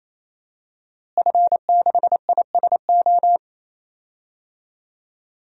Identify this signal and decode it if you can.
Morse code, heard as a keyed tone, about 28 wpm, 700 Hz. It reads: F6ISO
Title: DXCC-100 Call Signs - 5 Characters - Repeated 3x - 28wpm